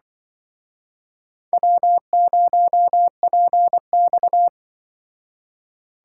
Morse code, heard as a keyed tone, about 25 wpm, 700 Hz. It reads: W0PX
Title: Speed Racing - USA 1x2 Call Signs 20wpm